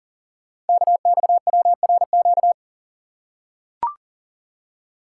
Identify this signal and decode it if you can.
Morse code, heard as a keyed tone, about 39 wpm, 700 Hz. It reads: KXWRQ
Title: Code Groups - 5 Letters - 40wpm